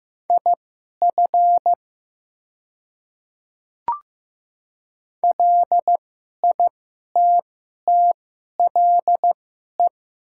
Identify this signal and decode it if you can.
Morse code, heard as a keyed tone, about 15 wpm, spaced as for 11 wpm, 700 Hz. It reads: IF LITTLE
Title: 2x Character Spacing - Top 200 Words - 15wpm